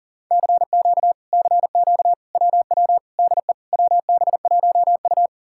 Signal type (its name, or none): Morse code